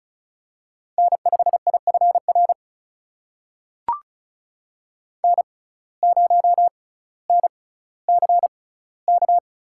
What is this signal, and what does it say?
Morse code, heard as a keyed tone, about 36 wpm, spaced as for 18 wpm, 700 Hz. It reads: N5IFR N0NCK
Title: USA Call Signs - 6x Character Spacing - 35wpm